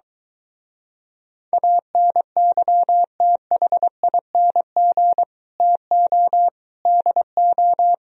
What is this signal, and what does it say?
Morse code, heard as a keyed tone, about 22 wpm, 700 Hz. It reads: ANYTHING TO DO